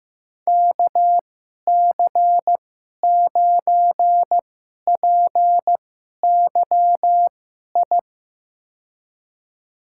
Morse code, heard as a keyed tone, about 15 wpm, spaced as for 11 wpm, 700 Hz. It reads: KC9PYI
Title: USA Call Signs - 6 Characters - 2x Character Spacing - 15wpm